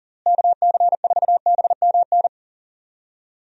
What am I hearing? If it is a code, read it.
Morse code, heard as a keyed tone, about 40 wpm, 700 Hz. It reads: KC4BMN
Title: USA Call Signs - 6 Characters - Repeated 3x - 40wpm